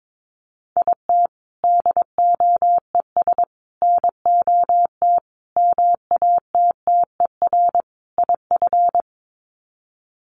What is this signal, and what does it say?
Morse code, heard as a keyed tone, about 22 wpm, 700 Hz. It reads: IT DOES NOT MATTER IF
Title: Sets of 5 Words Encore 22wpm